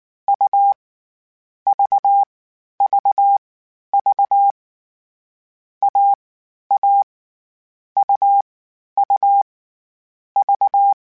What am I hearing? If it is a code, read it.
Morse code, heard as a keyed tone, about 19 wpm, 800 Hz. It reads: U V V V A A U U V
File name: Difficult Combos / Difficult Combos-mp3s / AUV AUV4 @18